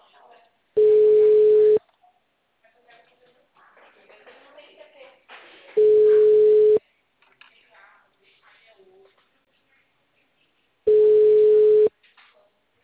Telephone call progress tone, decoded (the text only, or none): ringback tone